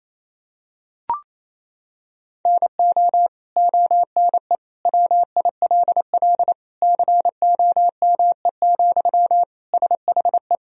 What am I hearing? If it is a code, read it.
Morse code, heard as a keyed tone, about 28 wpm, 700 Hz. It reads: NO ONE WILL COME, SHE
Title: Sentences from Top 100 Words and Abbreviations 28wpm